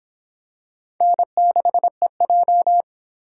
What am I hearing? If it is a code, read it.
Morse code, heard as a keyed tone, about 23 wpm, 700 Hz. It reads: N6EJ